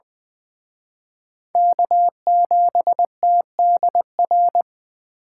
Morse code, heard as a keyed tone, about 24 wpm, 700 Hz. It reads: K7TDR
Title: Speed Racing - USA 1x3 Call Signs 20wpm